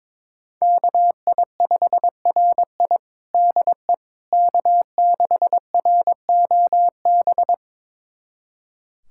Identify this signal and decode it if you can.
Morse code, heard as a keyed tone, about 23 wpm, 700 Hz. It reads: KI5RI DE K6ROB
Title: USA Callsign DE Callsign - 5 Characters - 22wpm